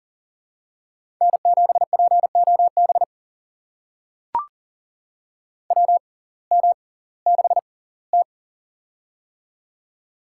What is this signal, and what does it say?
Morse code, heard as a keyed tone, about 40 wpm, spaced as for 22 wpm, 700 Hz. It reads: N7POB WM6T
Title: USA Call Signs - 6x Character Spacing - 40wpm